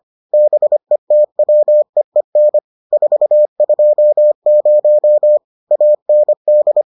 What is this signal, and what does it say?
Morse code, heard as a keyed tone, about 25 wpm, 600 Hz. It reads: BETWEEN 420 AND